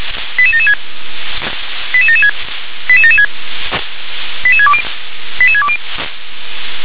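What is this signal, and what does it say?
Five-tone selective call (ZVEI), heard as quick runs of five tones, tones of about 70 ms. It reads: 88776 88776 88776 88620 88620
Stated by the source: ZVEI five-tone selective call, code 88771, received by SDR.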